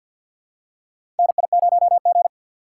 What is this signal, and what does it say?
Morse code, heard as a keyed tone, about 52 wpm, 700 Hz. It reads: NI0G
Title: USA Call Signs - 3x Character Spacing - 50wpm